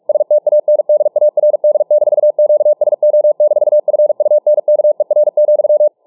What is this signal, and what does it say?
Morse code, heard as a keyed tone, about 48 wpm, 600 Hz. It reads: STANDARD-QSO-FUNKER,